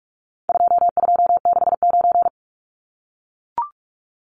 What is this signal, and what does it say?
Morse code, heard as a keyed tone, about 44 wpm, 700 Hz. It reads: 2269
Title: Numbers - 4-Digit 45wpm